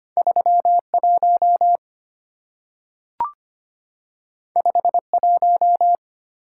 Morse code, heard as a keyed tone, about 25 wpm, 700 Hz. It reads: 3 1 5 1
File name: Two Letter-Number Combinations - 25wpm